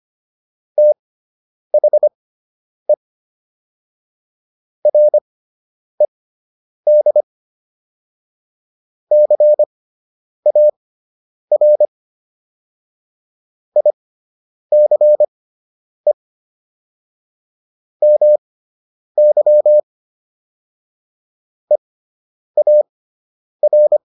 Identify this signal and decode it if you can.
Morse code, heard as a keyed tone, about 25 wpm, spaced as for 9 wpm, 600 Hz. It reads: THE RED CAR ICE MY EAR